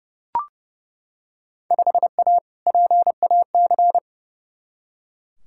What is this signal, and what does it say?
Morse code, heard as a keyed tone, about 30 wpm, 700 Hz. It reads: 5A PAC
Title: Field Day Exchanges Encore - 30wpm